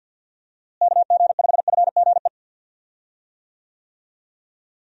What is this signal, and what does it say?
Morse code, heard as a keyed tone, about 51 wpm, 700 Hz. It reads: KG5FGE
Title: USA Call Signs - 6 Characters - Repeated 2x - 50wpm